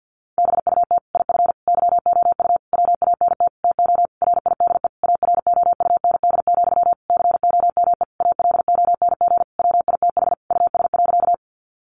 Morse code, heard as a keyed tone, about 50 wpm, 700 Hz. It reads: BUT IF YOU WANT TO RIDE AROUND, COME ALONG WITH US.
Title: Sentences from Top 1200 Words 50wpm